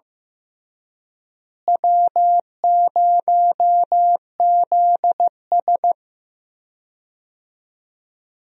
Morse code, heard as a keyed tone, about 15 wpm, 700 Hz. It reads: W0ZS
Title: USA Call Signs - 4 Characters - Repeated 3x - 15wpm